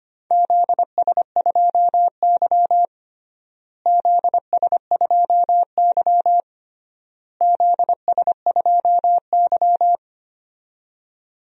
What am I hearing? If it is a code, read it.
Morse code, heard as a keyed tone, about 25 wpm, 700 Hz. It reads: ZS2Y ZS2Y ZS2Y
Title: DXCC-100 Call Signs - 4 Characters - Repeated 3x - 25wpm